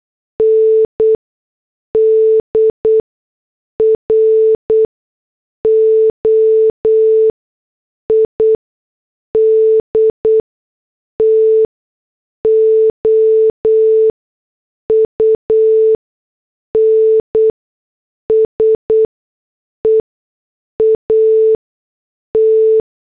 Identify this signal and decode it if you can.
Morse code, heard as a keyed tone, about 8 wpm, spaced as for 6 wpm, 440 Hz. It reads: NDROID TO UNSEAT
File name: morse / headlines